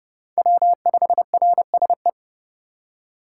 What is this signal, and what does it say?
Morse code, heard as a keyed tone, about 29 wpm, 700 Hz. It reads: W5RSE